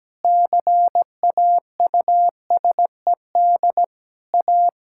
Morse code, heard as a keyed tone, about 17 wpm, 700 Hz. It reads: CAUSED A
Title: Pangram Challenge v2 - 17wpm